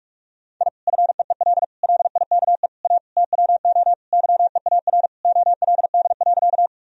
Morse code, heard as a keyed tone, about 41 wpm, 700 Hz. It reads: I FEEL LIKE A TWO YEAR OLD.
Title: Sentences from Top 1000 Words - Review Part 1 - 45wpm